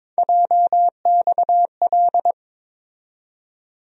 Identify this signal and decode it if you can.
Morse code, heard as a keyed tone, about 22 wpm, 700 Hz. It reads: JXL
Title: USA Call Sign Postfixes - 3 Letters 22wpm